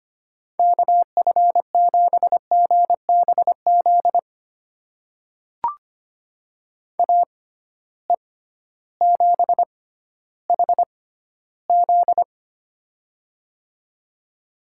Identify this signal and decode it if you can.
Morse code, heard as a keyed tone, about 25 wpm, spaced as for 14 wpm, 700 Hz. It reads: KF7GBZ AE7HZ
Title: USA Call Signs - 6x Character Spacing - 25wpm